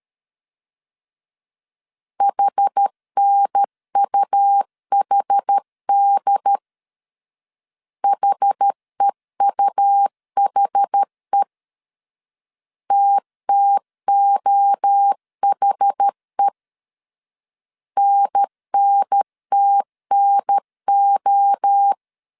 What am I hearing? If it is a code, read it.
Morse code, heard as a keyed tone, about 13 wpm, 800 Hz. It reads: HNUHD HEUHE TTOHE NNTNO
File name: CW / DET / 1 3 1500C S A 12wpm 2905